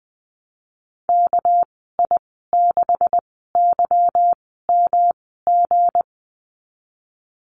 Morse code, heard as a keyed tone, about 20 wpm, spaced as for 14 wpm, 700 Hz. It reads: KI6YMG
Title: 238 Lesson 38g - Letter q - Call Signs 20wpm